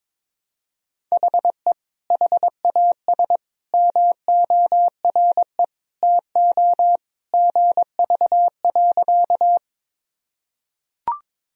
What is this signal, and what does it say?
Morse code, heard as a keyed tone, about 21 wpm, 700 Hz. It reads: HE HAS MORE TO GV.
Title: Sentences from Top 200 Words and Abbreviations 22wpm